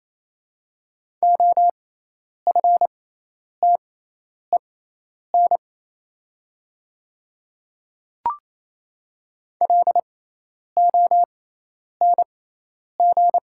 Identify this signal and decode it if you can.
Morse code, heard as a keyed tone, about 27 wpm, spaced as for 10 wpm, 690 Hz. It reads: OFTEN LONG
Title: 6x Character Spacing - Top 500 Words - 28wpm